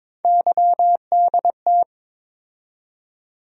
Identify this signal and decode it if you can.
Morse code, heard as a keyed tone, about 22 wpm, 700 Hz. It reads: YDT
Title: USA Call Sign Postfixes - 3 Letters 22wpm